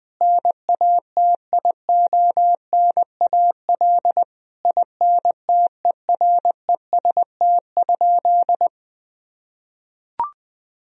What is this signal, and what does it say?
Morse code, heard as a keyed tone, about 20 wpm, 700 Hz. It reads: NATIONAL INTEREST?
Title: Sentences from Top 1000 Words - Review Part 4 - 20wpm